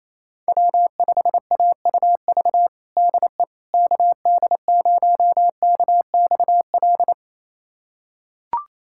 Morse code, heard as a keyed tone, about 27 wpm, 700 Hz. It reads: W5AUV DE KD0KXL
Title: USA Callsign DE Callsign 28wpm